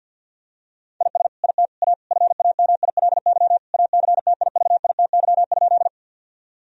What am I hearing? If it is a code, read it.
Morse code, heard as a keyed tone, about 50 wpm, 700 Hz. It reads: IS IT A FAMILY ACTIVITY?
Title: Sentences from Top 1000 Words - Review Part 4 - 50wpm